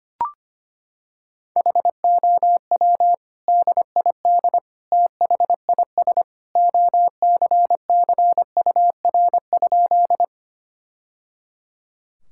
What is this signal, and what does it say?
Morse code, heard as a keyed tone, about 25 wpm, 700 Hz. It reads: HOW DID THIS OCCUR?
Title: Sentences from Top 1000 Words - Review Part 6 - 25wpm